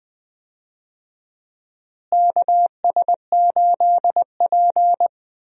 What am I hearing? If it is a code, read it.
Morse code, heard as a keyed tone, about 20 wpm, 700 Hz. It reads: KS8P
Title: USA Call Signs - 4 Characters - Repeated 3x - 20wpm